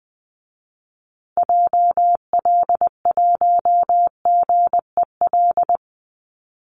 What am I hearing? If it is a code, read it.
Morse code, heard as a keyed tone, about 20 wpm, 700 Hz. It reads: JL1GEL
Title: Call Signs Repeated 3x - 20wpm